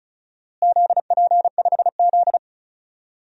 Morse code, heard as a keyed tone, about 34 wpm, 690 Hz. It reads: ZP5Z
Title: DXCC-100 Call Signs - 4 Characters - Repeated 2x - 35wpm